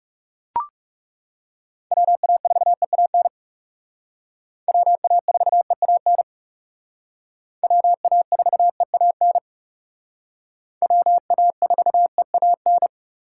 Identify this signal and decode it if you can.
Morse code, heard as a keyed tone, about 37 wpm, 700 Hz. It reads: WA4EAN WA4EAN WA4EAN WA4EAN
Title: Speed Racing - USA 2x3 Call Signs 30wpm